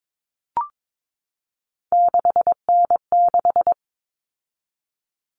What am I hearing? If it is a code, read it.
Morse code, heard as a keyed tone, about 22 wpm, 700 Hz. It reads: 6N6
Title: All Cut Numbers - 3-Digit 22wpm